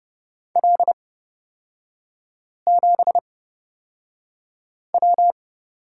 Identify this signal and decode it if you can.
Morse code, heard as a keyed tone, about 30 wpm, spaced as for 5 wpm, 700 Hz. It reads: L7W
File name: Lesson 37e - Letter x - Three Characters 30wpm